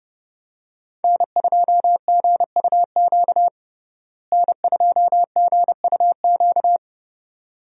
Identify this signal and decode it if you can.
Morse code, heard as a keyed tone, about 30 wpm, 700 Hz. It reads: N2GUQ N2GUQ
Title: USA Call Signs - 5 Characters - Repeated 2x - 30wpm